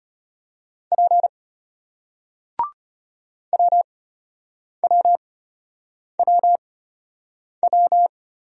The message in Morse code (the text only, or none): P WWWW